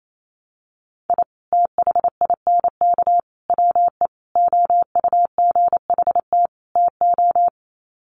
Morse code, heard as a keyed tone, about 28 wpm, 700 Hz. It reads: I THINK WE OUGHT TO
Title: Sets of 5 Words Encore 28wpm